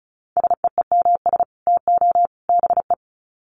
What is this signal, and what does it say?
Morse code, heard as a keyed tone, about 34 wpm, 700 Hz. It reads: SEEMS TO BE
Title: Sets of 3 Words 35wpm